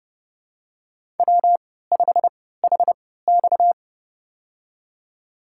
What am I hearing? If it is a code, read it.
Morse code, heard as a keyed tone, about 31 wpm, spaced as for 17 wpm, 700 Hz. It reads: W5HX
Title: USA Call Signs - 4 Characters - 3x Character Spacing - 30wpm